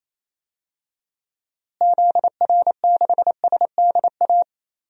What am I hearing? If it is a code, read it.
Morse code, heard as a keyed tone, about 28 wpm, 700 Hz. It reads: ZR6SDA
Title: DXCC-100 Call Signs - 6 Characters - Repeated 3x - 28wpm